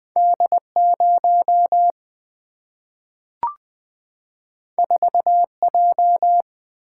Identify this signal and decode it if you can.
Morse code, heard as a keyed tone, about 20 wpm, 700 Hz. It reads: D0 4J
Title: Single-Letter Single-Number 20wpm